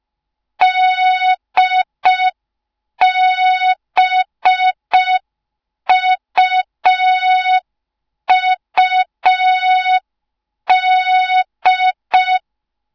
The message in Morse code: DBUUD